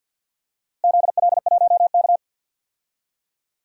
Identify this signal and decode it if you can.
Morse code, heard as a keyed tone, about 52 wpm, 700 Hz. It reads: ZL1K
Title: DXCC-100 Call Signs - 4 Characters - Repeated 2x - 50wpm